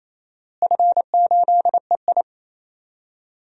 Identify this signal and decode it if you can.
Morse code, heard as a keyed tone, about 28 wpm, 700 Hz. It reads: F8EI
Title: DXCC-100 Call Signs - 4 Characters - Repeated 2x - 28wpm